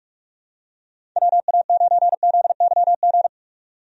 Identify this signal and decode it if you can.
Morse code, heard as a keyed tone, about 45 wpm, 700 Hz. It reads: WA9ZYG
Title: USA Call Signs - 6 Characters - 45wpm